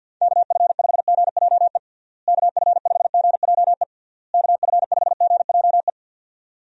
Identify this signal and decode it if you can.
Morse code, heard as a keyed tone, about 50 wpm, 700 Hz. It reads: KF5GJE KF5GJE KF5GJE
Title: USA Call Signs - 6 Characters - Repeated 3x - 50wpm